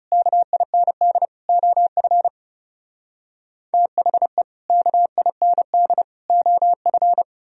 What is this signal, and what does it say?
Morse code, heard as a keyed tone, about 38 wpm, 700 Hz. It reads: KIND OF THE KIND OF